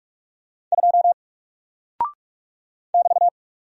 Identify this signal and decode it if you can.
Morse code, heard as a keyed tone, about 44 wpm, 700 Hz. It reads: 2 <BT>
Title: Single Character (Letter:Number:Punctuation:Prosign) 45wpm